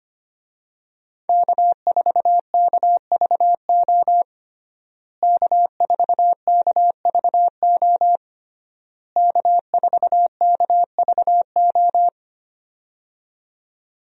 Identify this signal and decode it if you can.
Morse code, heard as a keyed tone, about 25 wpm, 700 Hz. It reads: K4KVO K4KVO K4KVO
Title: USA Call Signs - 5 Characters - Repeated 3x - 25wpm